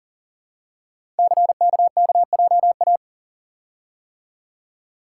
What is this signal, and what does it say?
Morse code, heard as a keyed tone, about 41 wpm, 690 Hz. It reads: CKKJA
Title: Code Groups - 5 Letters - 40wpm